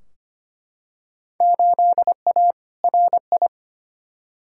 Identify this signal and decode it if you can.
Morse code, heard as a keyed tone, about 25 wpm, 700 Hz. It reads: 8A RI
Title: Field Day Exchanges Encore - 25wpm